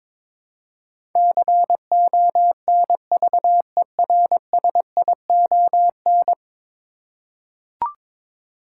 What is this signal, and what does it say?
Morse code, heard as a keyed tone, about 22 wpm, 700 Hz. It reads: CONVERSION
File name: practice / 22wpm / 10 Letter Words - First 3 Letters Given - 22wpm